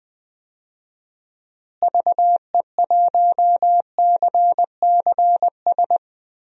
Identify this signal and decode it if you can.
Morse code, heard as a keyed tone, about 20 wpm, 700 Hz. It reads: VE1CCS
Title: DXCC-100 Call Signs - 6 Characters - Repeated 2x - 20wpm